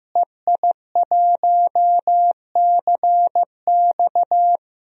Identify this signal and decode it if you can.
Morse code, heard as a keyed tone, about 15 wpm, 700 Hz. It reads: EI1CX
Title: DXCC-100 Call Signs - 5 Characters - Repeated 2x - 15wpm